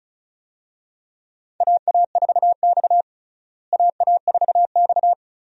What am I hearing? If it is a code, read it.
Morse code, heard as a keyed tone, about 36 wpm, 700 Hz. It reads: AA4X AA4X
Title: USA Call Signs - 4 Characters - Repeated 3x - 35wpm